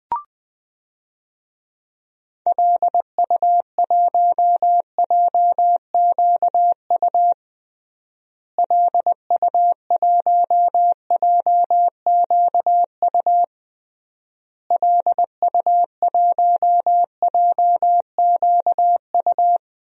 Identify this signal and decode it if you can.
Morse code, heard as a keyed tone, about 21 wpm, 700 Hz. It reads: LU1JQU LU1JQU LU1JQU
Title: DXCC-100 Call Signs - 6 Characters - Repeated 3x - 20wpm